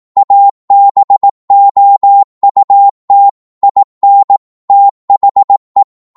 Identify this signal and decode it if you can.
Morse code, heard as a keyed tone, about 18 wpm, 800 Hz. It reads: ABOUT IN THE
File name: Morse / psg-18wpm